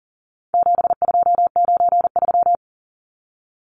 Morse code, heard as a keyed tone, about 40 wpm, 700 Hz. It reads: 7293
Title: Numbers - 4-Digit 40wpm